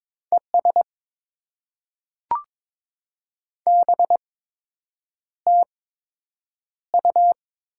Morse code, heard as a keyed tone, about 23 wpm, spaced as for 8 wpm, 700 Hz. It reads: ES BTU
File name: Top 20 QSO Elements - 8x Character Spacing - 22wpm